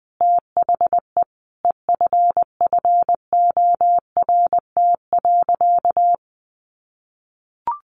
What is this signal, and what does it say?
Morse code, heard as a keyed tone, about 20 wpm, 700 Hz. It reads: THE EFFORT.
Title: Sentences from Top 1200 Words 20wpm